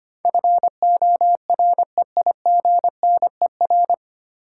Morse code, heard as a keyed tone, about 25 wpm, 700 Hz. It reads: FOREIGNER
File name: practice / 25wpm / 9 Letter Words - First 3 Letters Given - 25wpm